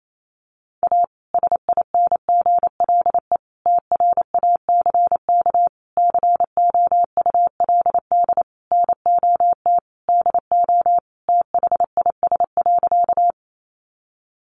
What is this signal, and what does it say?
Morse code, heard as a keyed tone, about 28 wpm, 700 Hz. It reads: A SINGLE TRACK COULD NOT DO THIS.
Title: Sentences from Top 1100 Words 28wpm